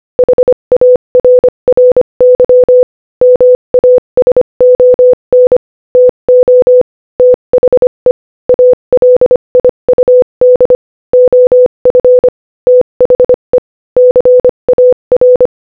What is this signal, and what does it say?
Morse code, heard as a keyed tone, about 25 wpm, 500 Hz. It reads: HARRY MASON TO THE ALIUD OF THE CAR
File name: Morse-Code-Audio-Speed-25-1